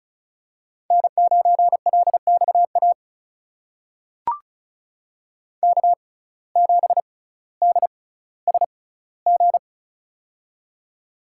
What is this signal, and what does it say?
Morse code, heard as a keyed tone, about 36 wpm, spaced as for 18 wpm, 700 Hz. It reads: N9LXA K7DSG
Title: USA Call Signs - 5 Characters - 6x Character Spacing - 35wpm